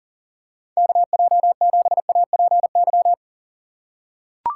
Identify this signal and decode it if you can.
Morse code, heard as a keyed tone, about 40 wpm, 700 Hz. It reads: KJ7APY
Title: USA Call Signs - 6 Characters - 2x Character Spacing - 40wpm